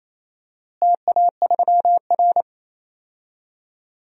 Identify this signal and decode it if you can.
Morse code, heard as a keyed tone, about 27 wpm, 700 Hz. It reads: TA3R
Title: DXCC-100 Call Signs - 4 Characters - Repeated 3x - 28wpm